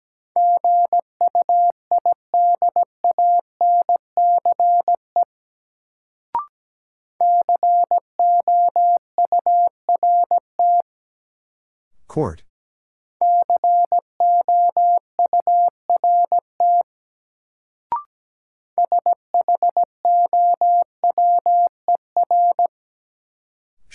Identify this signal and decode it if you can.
Morse code, heard as a keyed tone, about 17 wpm, 700 Hz. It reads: GUIDANCE COURT COURT SHOWER
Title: Sentences from Top 1600 Words - New Word Drill - 17wpm